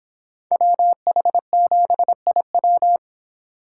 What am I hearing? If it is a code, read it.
Morse code, heard as a keyed tone, about 24 wpm, 700 Hz. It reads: WH7IW